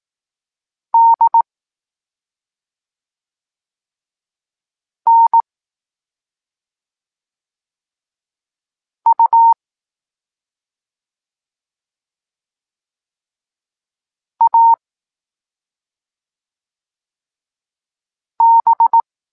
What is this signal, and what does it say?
Morse code, heard as a keyed tone, about 18 wpm, 920 Hz. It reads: D N U A B
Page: Tekst 14 – doskonalenie odbioru poznanych liter